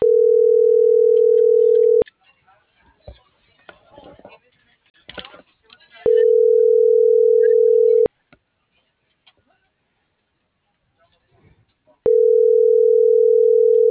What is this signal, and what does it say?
Ringback tone (North America), 440 + 480 Hz.
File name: CENTRAL SKYBLUE / out lorena.gonzalez